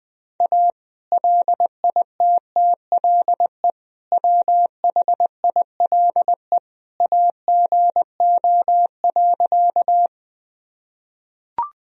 Morse code, heard as a keyed tone, about 20 wpm, 700 Hz. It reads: A LITTLE WHILE AGO.
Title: Sentences from Top 1000 Words - Review Part 3 - 20wpm